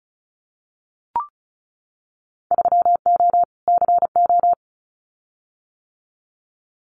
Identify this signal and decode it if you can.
Morse code, heard as a keyed tone, about 36 wpm, 700 Hz. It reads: 3O CO
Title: Winter Field Day Exchanges Encore 35wpm